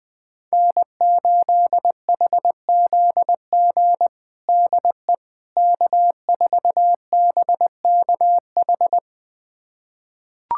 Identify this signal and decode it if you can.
Morse code, heard as a keyed tone, about 20 wpm, 700 Hz. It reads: N8HZG DE K4BKH
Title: USA Callsign DE Callsign 20wpm